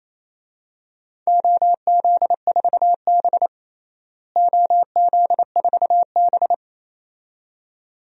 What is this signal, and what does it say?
Morse code, heard as a keyed tone, about 28 wpm, 700 Hz. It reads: OZ4B OZ4B
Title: DXCC-100 Call Signs - 4 Characters - Repeated 2x - 28wpm